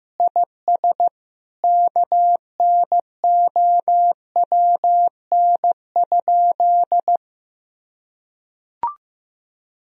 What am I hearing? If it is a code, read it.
Morse code, heard as a keyed tone, about 15 wpm, 700 Hz. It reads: IS KNOWN?
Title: Sentences from Top 1000 Words - Review Part 3 - 15wpm